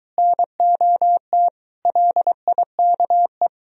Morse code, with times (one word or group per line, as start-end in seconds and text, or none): 0.2-1.5 NOT
1.9-3.5 LIKE